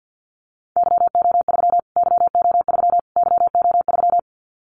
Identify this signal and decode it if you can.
Morse code, heard as a keyed tone, about 50 wpm, 700 Hz. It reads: YO3 YO3 YO3
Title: Callsign Prefixes for Top 25 DXCC - Repeated 3x - 50wpm